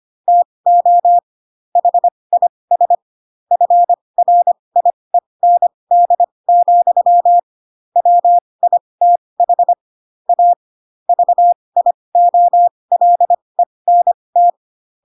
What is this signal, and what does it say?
Morse code, heard as a keyed tone, about 25 wpm, spaced as for 20 wpm, 700 Hz. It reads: TO HIS FRIEND, WITH A VIOLENT
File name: Pinocchio / Pinocchio ch2 15 25 ik1hge